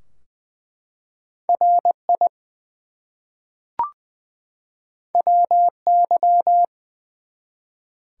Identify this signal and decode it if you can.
Morse code, heard as a keyed tone, about 20 wpm, 700 Hz. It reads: RI WY
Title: US States and Canadian Provinces 20wpm